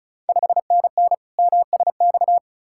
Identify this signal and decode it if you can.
Morse code, heard as a keyed tone, about 36 wpm, 690 Hz. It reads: 5NN MSX
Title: Ontario QSO Party Exchange 35wpm